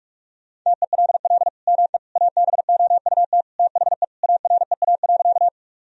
Morse code, heard as a keyed tone, about 41 wpm, 700 Hz. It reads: TELL ME ABOUT THE AREA.
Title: Sentences from Top 1000 Words - Review Part 2 - 45wpm